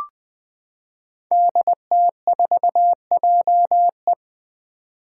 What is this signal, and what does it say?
Morse code, heard as a keyed tone, about 21 wpm, 700 Hz. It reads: DT4JE
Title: Five Challenging Letter-Number Combinations 20wpm